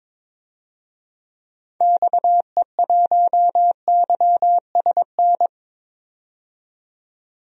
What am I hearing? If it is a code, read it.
Morse code, heard as a keyed tone, about 22 wpm, 700 Hz. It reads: XE1YSN
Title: DXCC-100 Call Signs - 6 Characters - 22wpm